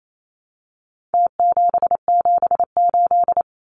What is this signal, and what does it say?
Morse code, heard as a keyed tone, about 28 wpm, 700 Hz. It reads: T778
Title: Numbers - Contest Style - 4-Digit 28wpm